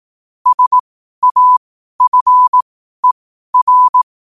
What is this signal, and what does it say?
Morse code, heard as a keyed tone, about 18 wpm, spaced as for 12 wpm, 1000 Hz. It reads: SAFER
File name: qotdfiles / 12wpm